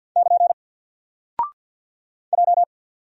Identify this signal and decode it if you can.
Morse code, heard as a keyed tone, about 49 wpm, 700 Hz. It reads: <KN> J
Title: Single Character (Letter:Number:Punctuation:Prosign) 50wpm